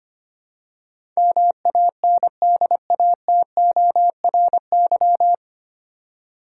Morse code, heard as a keyed tone, about 25 wpm, 700 Hz. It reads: MANDATORY
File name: Nine Letter Words - 25wpm